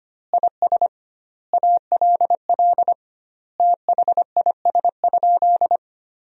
Morse code, heard as a keyed tone, about 25 wpm, 700 Hz. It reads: IS ALL THIS?
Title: 2x Word-Spacing - Sentences from Top 100 Words - 25wpm